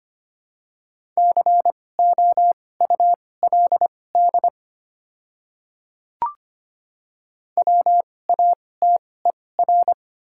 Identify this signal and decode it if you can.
Morse code, heard as a keyed tone, about 25 wpm, spaced as for 18 wpm, 700 Hz. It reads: COULD WATER
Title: 2x Character Spacing - Top 100 Words - 25wpm